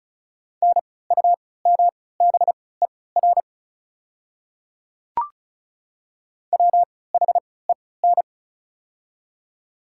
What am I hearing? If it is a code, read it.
Morse code, heard as a keyed tone, about 34 wpm, spaced as for 20 wpm, 690 Hz. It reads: NUMBER WHEN
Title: 3x Character Spacing - Top 100 Words - 35wpm